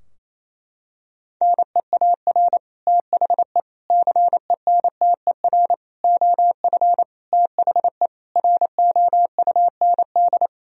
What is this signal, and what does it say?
Morse code, heard as a keyed tone, about 28 wpm, 700 Hz. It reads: NEAR THE CENTER OF THE ROUND
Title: Sentences from Top 1000 Words - Review Part 2 - 28wpm